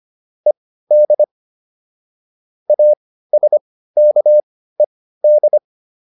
Morse code, heard as a keyed tone, about 25 wpm, spaced as for 15 wpm, 600 Hz. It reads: ED ASKED